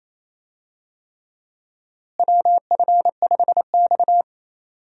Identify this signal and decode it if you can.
Morse code, heard as a keyed tone, about 28 wpm, 700 Hz. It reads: WF5X
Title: USA Call Signs - 4 Characters - Repeated 3x - 28wpm